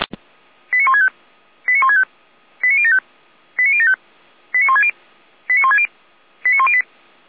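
Five-tone selective call (ZVEI), heard as quick runs of five tones, tones of about 70 ms. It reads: 89165 89165 89075 89075 89160 89160 89108
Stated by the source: ZVEI five-tone selective call 89560 received on SDR.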